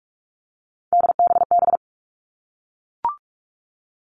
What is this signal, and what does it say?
Morse code, heard as a keyed tone, about 44 wpm, 700 Hz. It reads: DBB
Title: All Cut Numbers - 3-Digit 45wpm